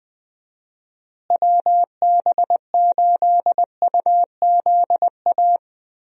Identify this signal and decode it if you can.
Morse code, heard as a keyed tone, about 20 wpm, 700 Hz. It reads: WB8UZA
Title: USA Call Signs - 6 Characters - Repeated 2x - 20wpm